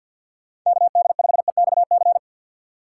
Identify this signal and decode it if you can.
Morse code, heard as a keyed tone, about 50 wpm, 700 Hz. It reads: KD5EXC
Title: USA Call Signs - 2x Character Spacing - 50wpm